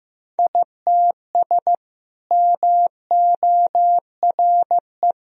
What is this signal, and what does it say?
Morse code, heard as a keyed tone, about 15 wpm, 700 Hz. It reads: ITS MORE
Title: Sentences from Top 1000 Words - Review Part 1 - 15wpm